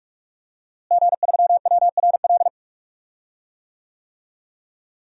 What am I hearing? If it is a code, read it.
Morse code, heard as a keyed tone, about 46 wpm, 700 Hz. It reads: G3WRL
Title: DXCC-100 Call Signs - 5 Characters - Repeated 2x - 45wpm